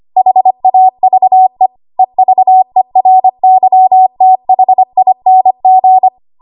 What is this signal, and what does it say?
Morse code, heard as a keyed tone, about 25 wpm, 750 Hz. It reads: HAVE EVERYTHING
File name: Morse / 25WPM